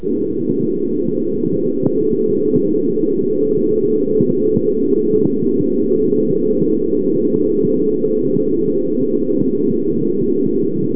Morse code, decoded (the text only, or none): none